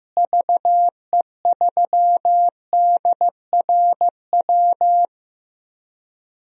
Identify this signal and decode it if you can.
Morse code, heard as a keyed tone, about 15 wpm, 700 Hz. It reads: VE3DRW
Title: DXCC-100 Call Signs - 6 Characters - Repeated 3x - 15wpm